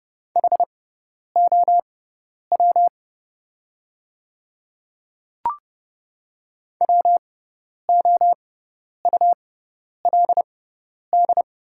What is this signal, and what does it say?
Morse code, heard as a keyed tone, about 31 wpm, spaced as for 10 wpm, 700 Hz. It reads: HOW WOULD
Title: 6x Character Spacing - Top 100 Words - 30wpm